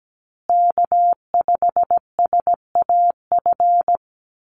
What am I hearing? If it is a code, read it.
Morse code, heard as a keyed tone, about 17 wpm, 700 Hz. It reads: K5SAF
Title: USA Call Signs Repeated 2x 17wpm